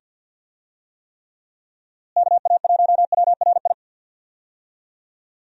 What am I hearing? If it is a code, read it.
Morse code, heard as a keyed tone, about 46 wpm, 700 Hz. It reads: KA1WRI
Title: USA Call Signs - 6 Characters - Repeated 2x - 50wpm